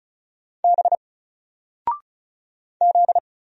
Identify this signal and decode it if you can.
Morse code, heard as a keyed tone, about 36 wpm, 700 Hz. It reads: B Z
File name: Single-Letter-Number-35wpm